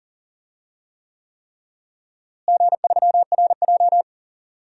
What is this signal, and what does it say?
Morse code, heard as a keyed tone, about 40 wpm, 700 Hz. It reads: G3RJ
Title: DXCC-100 Call Signs - 4 Characters - Repeated 3x - 40wpm